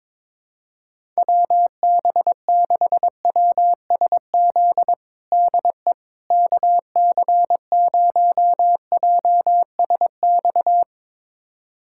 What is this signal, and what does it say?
Morse code, heard as a keyed tone, about 22 wpm, 700 Hz. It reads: WB6WSZ DE KC0JSX
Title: USA Callsign DE Callsign 22wpm